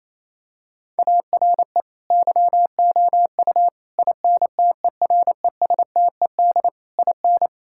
Morse code, heard as a keyed tone, about 28 wpm, 700 Hz. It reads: ARE YOU INTERESTED IN
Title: Sentences from Top 1000 Words - Review Part 6 - 28wpm